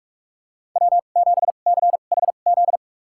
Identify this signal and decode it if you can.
Morse code, heard as a keyed tone, about 46 wpm, spaced as for 32 wpm, 690 Hz. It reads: W7CHZ
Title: USA Call Signs - 5 Characters - 2x Character Spacing - 45wpm